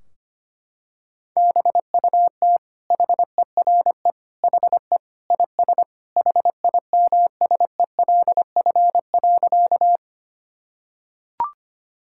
Morse code, heard as a keyed tone, about 25 wpm, 700 Hz. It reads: BUT HERE HE IS HIMSELF.
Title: Sentences from Top 1600 Words 25wpm